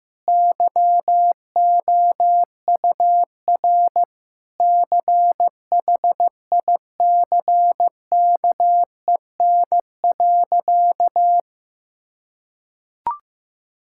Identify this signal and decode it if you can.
Morse code, heard as a keyed tone, about 16 wpm, 700 Hz. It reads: YOUR CHICKEN.
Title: Sentences from Top 1000 Words - Review Part 5 - 15wpm